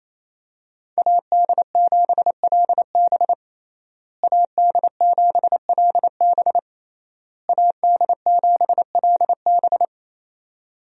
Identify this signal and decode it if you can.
Morse code, heard as a keyed tone, about 28 wpm, 700 Hz. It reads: AD7LB AD7LB AD7LB
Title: USA Call Signs - 5 Characters - Repeated 3x - 28wpm